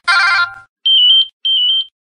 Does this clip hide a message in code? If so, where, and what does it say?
No.